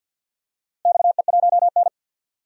Morse code, heard as a keyed tone, about 47 wpm, 700 Hz. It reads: XE1N
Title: DXCC-100 Call Signs - 4 Characters - Repeated 2x - 50wpm